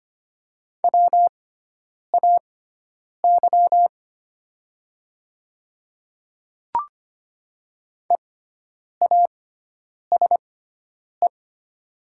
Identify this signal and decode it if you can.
Morse code, heard as a keyed tone, about 25 wpm, spaced as for 9 wpm, 700 Hz. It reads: WAY EASE
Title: 6x Character Spacing - Top 300 Words - 25wpm